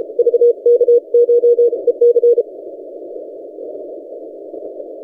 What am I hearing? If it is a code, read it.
Morse code, heard as a keyed tone, about 31 wpm, 480 Hz. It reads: VKCH EC
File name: posts / vk9ec-40cw- -1410z